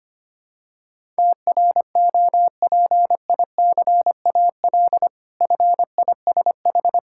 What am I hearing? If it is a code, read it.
Morse code, heard as a keyed tone, about 25 wpm, 700 Hz. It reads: TROPICAL FISH
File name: practice / 25wpm / Difficult Words with Hints - 25wpm